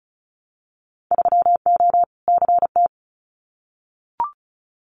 Morse code, heard as a keyed tone, about 36 wpm, 700 Hz. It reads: 3O CT
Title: Winter Field Day Exchanges Encore 35wpm